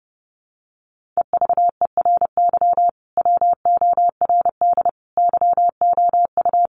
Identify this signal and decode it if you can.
Morse code, heard as a keyed tone, about 30 wpm, 700 Hz. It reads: EVERY WORD YOU
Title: Sentences from Top 400 Words 30wpm